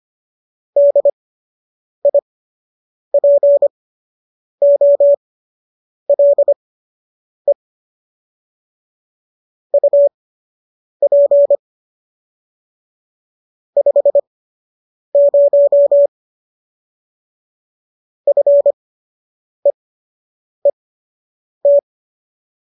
Morse code, heard as a keyed tone, about 25 wpm, spaced as for 8 wpm, 570 Hz. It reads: DIPOLE UP 50 FEET